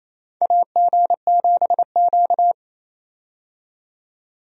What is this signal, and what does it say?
Morse code, heard as a keyed tone, about 28 wpm, 700 Hz. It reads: AG7Q
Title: USA Call Signs - 4 Characters - 28wpm